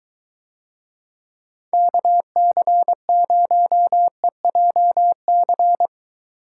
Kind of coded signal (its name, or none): Morse code